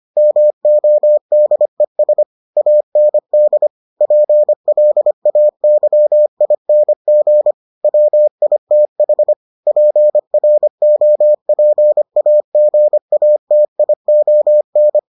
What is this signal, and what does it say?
Morse code, heard as a keyed tone, about 25 wpm, 600 Hz. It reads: MODES AND PLAYING WITH PROPAGATION